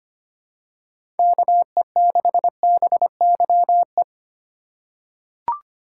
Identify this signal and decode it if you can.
Morse code, heard as a keyed tone, about 25 wpm, 700 Hz. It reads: KE6BYE
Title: USA Call Signs - 6 Characters - 3x Character Spacing - 25wpm